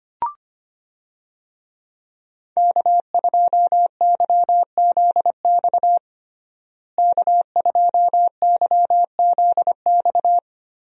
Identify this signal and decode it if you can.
Morse code, heard as a keyed tone, about 25 wpm, 700 Hz. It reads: K2YZX K2YZX
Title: USA Call Signs - 5 Characters - Repeated 3x - 25wpm